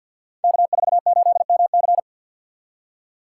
Morse code, heard as a keyed tone, about 50 wpm, 700 Hz. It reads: K48MC
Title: Code Groups - 5 Letter:Numbers - 50wpm